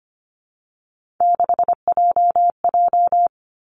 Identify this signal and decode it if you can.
Morse code, heard as a keyed tone, about 25 wpm, 700 Hz. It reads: 6JJ
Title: Three Challenging Letter-Number Combinations 25wpm